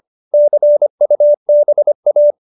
Morse code, heard as a keyed tone, about 25 wpm, 590 Hz. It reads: CUBA
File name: QSO304 25